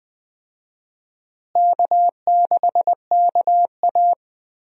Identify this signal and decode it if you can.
Morse code, heard as a keyed tone, about 20 wpm, 700 Hz. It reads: K6KA
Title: USA Call Signs - 4 Characters - Repeated 2x - 20wpm